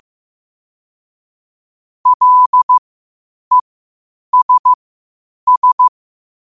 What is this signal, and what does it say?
Morse code, heard as a keyed tone, about 15 wpm, spaced as for 8 wpm, 1000 Hz. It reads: LESS
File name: qotdfiles / 8wpm